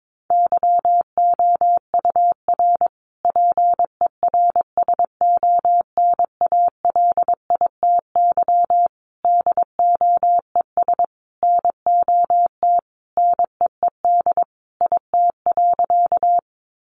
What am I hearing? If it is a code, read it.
Morse code, heard as a keyed tone, about 22 wpm, 700 Hz. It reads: YOUR PERSONALITY DOES NOT NEED IT.
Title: Sentences from Top 1400 Words 22wpm